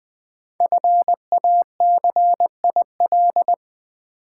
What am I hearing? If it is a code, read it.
Morse code, heard as a keyed tone, about 20 wpm, 700 Hz. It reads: FACIL
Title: Build A Word - 5 Letters At A Time - 10-Letter Words - 20wpm